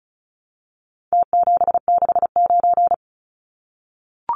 Morse code, heard as a keyed tone, about 34 wpm, 700 Hz. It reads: T769
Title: Numbers - Contest Style - 4-Digit 35wpm